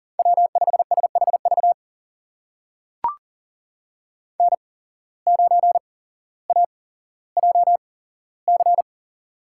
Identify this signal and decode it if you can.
Morse code, heard as a keyed tone, about 40 wpm, spaced as for 17 wpm, 700 Hz. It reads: W5SHV N9AJC
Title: USA Call Signs - 5 Characters - 8x Character Spacing - 40wpm